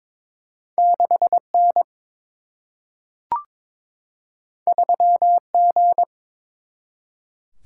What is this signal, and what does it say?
Morse code, heard as a keyed tone, about 22 wpm, 700 Hz. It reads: 6 N 3 G
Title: Single-Letter Single-Number 22wpm